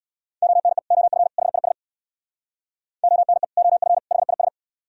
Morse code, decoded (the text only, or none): ONE OF HIS ONE OF HIS